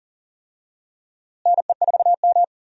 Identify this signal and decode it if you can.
Morse code, heard as a keyed tone, about 40 wpm, 700 Hz. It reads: NE4M
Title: USA Call Signs - 4 Characters - Repeated 3x - 40wpm